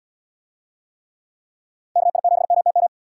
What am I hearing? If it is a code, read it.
Morse code, heard as a keyed tone, about 72 wpm, 700 Hz. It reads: KI7MTM